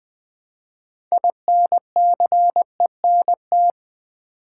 Morse code, heard as a keyed tone, about 20 wpm, 700 Hz. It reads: INCENT
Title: Build A Word - 3 Letters At A Time - 9-Letter Words - 20wpm